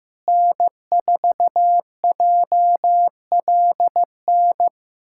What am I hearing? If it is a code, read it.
Morse code, heard as a keyed tone, about 15 wpm, 700 Hz. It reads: N4JLN
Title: USA Callsign DE Callsign - 5 Characters - 15wpm